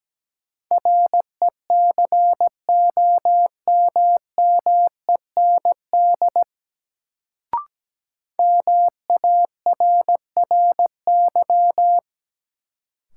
Morse code, heard as a keyed tone, about 17 wpm, 700 Hz. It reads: RECOMMEND MARRY
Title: Sentences from Top 1200 Words - New Word Drill - 17wpm